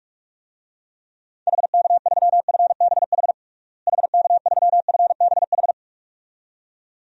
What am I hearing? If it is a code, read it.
Morse code, heard as a keyed tone, about 43 wpm, 700 Hz. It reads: HK3FBH HK3FBH
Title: DXCC-100 Call Signs - 6 Characters - Repeated 2x - 45wpm